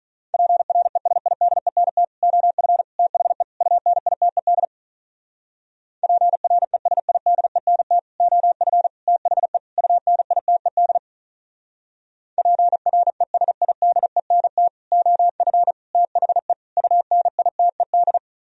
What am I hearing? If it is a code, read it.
Morse code, heard as a keyed tone, about 44 wpm, 700 Hz. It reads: PRESIDENT OF THE UNITED PRESIDENT OF THE UNITED PRESIDENT OF THE UNITED